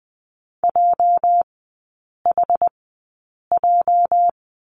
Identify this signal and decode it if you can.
Morse code, heard as a keyed tone, about 19 wpm, spaced as for 8 wpm, 690 Hz. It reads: JHJ
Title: 205 Lesson 33e - Letter j - Three Characters 20wpm